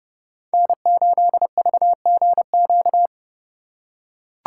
Morse code, heard as a keyed tone, about 29 wpm, 700 Hz. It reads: N8VGQ